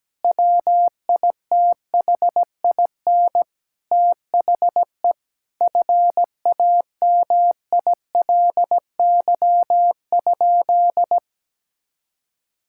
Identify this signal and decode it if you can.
Morse code, heard as a keyed tone, about 17 wpm, 700 Hz. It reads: WITHIN THE FAMILY?
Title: Sentences from Top 1600 Words 17wpm